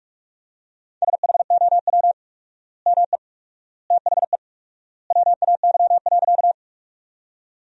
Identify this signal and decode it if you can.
Morse code, heard as a keyed tone, about 47 wpm, 700 Hz. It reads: SHOW ME THE WAY.
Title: 4x Word-Spacing - Sentences from Top 200 Words - 45wpm